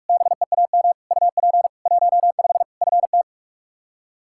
Morse code, heard as a keyed tone, about 46 wpm, 700 Hz. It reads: BEAM UP 15 FT
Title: K1USN SST QSO Practice 45wpm